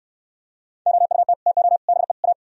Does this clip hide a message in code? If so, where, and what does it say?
Yes: Morse code from 0.9 to 2.3 s, heard as a keyed tone, about 65 wpm, 700 Hz. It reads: OUT TO BE A